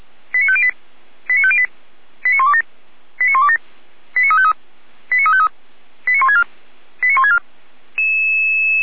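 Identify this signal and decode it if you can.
Five-tone selective call (ZVEI), heard as quick runs of five tones, tones of about 70 ms. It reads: 89508 89508 89127 89127 89353 89353 89164 89164
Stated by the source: ZVEI five-tone selective call received on SDR, from feuerwehr-orlamuende.org (Orlamünde fire brigade).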